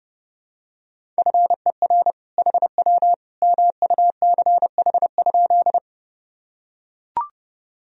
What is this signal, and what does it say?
Morse code, heard as a keyed tone, about 30 wpm, 700 Hz. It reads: FER HW MUCH?
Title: Sentences from Top 200 Words and Abbreviations 30wpm